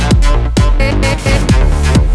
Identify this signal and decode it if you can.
There is no code.